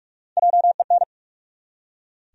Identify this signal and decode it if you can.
Morse code, heard as a keyed tone, about 47 wpm, 700 Hz. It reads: JEN
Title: USA Common Names - Max 3 Letters - 45wpm